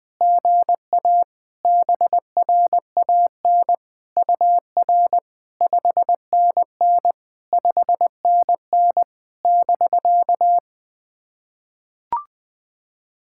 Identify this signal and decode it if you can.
Morse code, heard as a keyed tone, about 21 wpm, 700 Hz. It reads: GA BRAN UR 5NN 5NN <BK>
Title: SOTA QSO Practice 20wpm